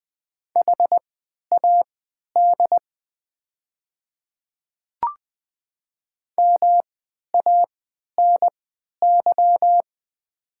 Morse code, heard as a keyed tone, about 20 wpm, spaced as for 11 wpm, 700 Hz. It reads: HAD MANY
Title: 3x Character Spacing - Top 100 Words - 20wpm